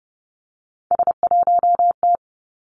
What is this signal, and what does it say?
Morse code, heard as a keyed tone, about 30 wpm, 700 Hz. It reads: S1T
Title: Three Challenging Letter-Number Combinations 30wpm